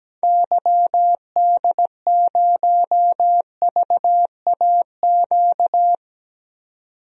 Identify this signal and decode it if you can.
Morse code, heard as a keyed tone, about 17 wpm, 700 Hz. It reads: YD0VAQ
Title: DXCC-100 Call Signs - 6 Characters - 17wpm